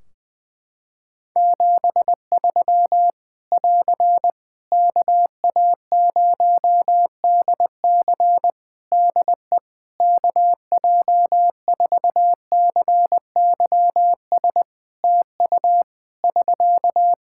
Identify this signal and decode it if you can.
Morse code, heard as a keyed tone, about 21 wpm, 700 Hz. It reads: 73 <AR> KA0DC DE KJ4CYS TU <SK>
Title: K1USN SST QSO Practice 20wpm